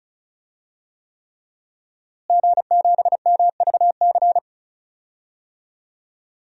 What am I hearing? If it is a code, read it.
Morse code, heard as a keyed tone, about 34 wpm, 700 Hz. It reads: G7MVC
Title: DXCC-100 Call Signs - 5 Characters - Repeated 3x - 35wpm